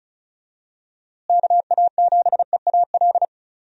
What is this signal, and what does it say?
Morse code, heard as a keyed tone, about 36 wpm, 700 Hz. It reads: KA7EAL
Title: USA Call Signs - 6 Characters - Repeated 3x - 35wpm